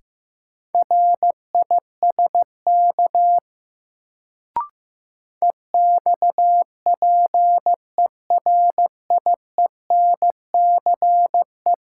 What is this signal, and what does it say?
Morse code, heard as a keyed tone, about 15 wpm, 700 Hz. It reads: RISK EXPERIENCE
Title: Sentences from Top 600 Words - New Word Drill - 15wpm